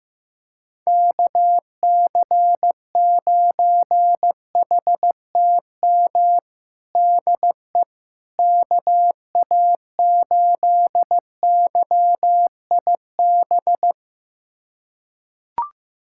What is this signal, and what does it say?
Morse code, heard as a keyed tone, about 15 wpm, 700 Hz. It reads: KC9HTM DE KA8YIB
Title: USA Callsign DE Callsign 15wpm